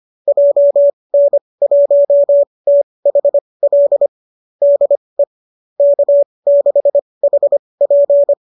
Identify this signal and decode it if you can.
Morse code, heard as a keyed tone, about 25 wpm, spaced as for 20 wpm, 570 Hz. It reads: JN1THL DE K6HP
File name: qso205 20